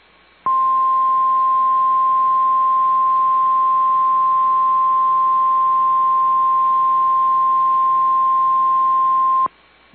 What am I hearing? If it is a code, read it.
Emergency Alert System (SAME) tones (the 1050 Hz attention tone). It is heard from 0.4 to 9.5 s.
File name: EAS Logs / KMNR